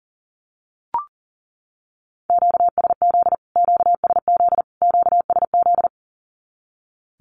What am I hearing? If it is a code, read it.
Morse code, heard as a keyed tone, about 40 wpm, 700 Hz. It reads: QSZ QSZ QSZ
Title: Q-Codes Repeated 3x 40wpm